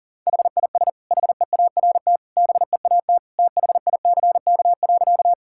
Morse code, heard as a keyed tone, about 40 wpm, 700 Hz. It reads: HIS HEART BEAT THICK.
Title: Sentences from Top 1600 Words 40wpm